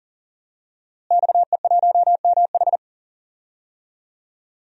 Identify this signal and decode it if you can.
Morse code, heard as a keyed tone, about 40 wpm, 700 Hz. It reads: XE1MH
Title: DXCC-100 Call Signs - 5 Characters - Repeated 3x - 40wpm